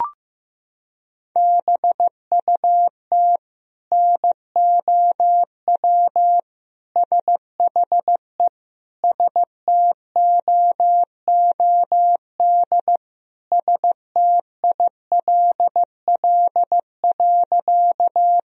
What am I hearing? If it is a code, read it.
Morse code, heard as a keyed tone, about 15 wpm, 700 Hz. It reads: BUT NOW SHE STOOD STILL.
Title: Sentences from Top 1000 Words - Review Part 2 - 15wpm